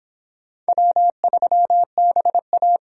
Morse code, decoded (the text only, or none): W3BA